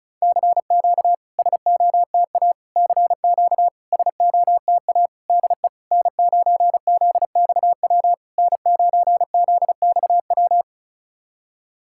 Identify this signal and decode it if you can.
Morse code, heard as a keyed tone, about 36 wpm, 700 Hz. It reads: CQ SOTA CQ SOTA DE N9ZXW N9ZXW
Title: SOTA QSO Practice 35wpm